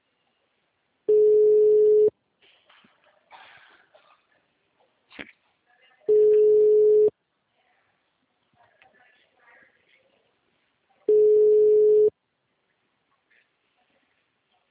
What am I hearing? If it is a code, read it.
Ringback tone (Europe, 425 Hz).
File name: out- -2007- - - .45334